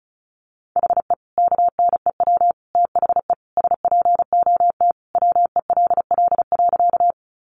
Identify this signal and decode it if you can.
Morse code, heard as a keyed tone, about 34 wpm, 700 Hz. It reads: HE KNEW THE SPOT WELL.
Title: Sentences from Top 1100 Words 35wpm